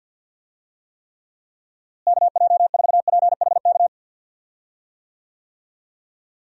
Morse code, heard as a keyed tone, about 50 wpm, 700 Hz. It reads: KJ4PHK
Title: USA Call Signs - 6 Characters - Repeated 3x - 50wpm